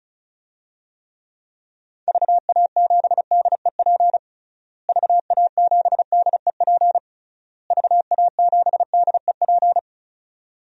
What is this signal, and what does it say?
Morse code, heard as a keyed tone, about 34 wpm, 700 Hz. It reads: VA7DEP VA7DEP VA7DEP
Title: DXCC-100 Call Signs - 6 Characters - Repeated 3x - 35wpm